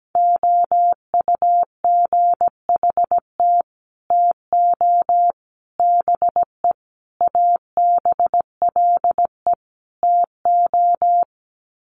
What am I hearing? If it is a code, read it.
Morse code, heard as a keyed tone, about 17 wpm, 700 Hz. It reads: OUGHT TO BE ABLE TO
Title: Sets of 5 Words Encore2 17wpm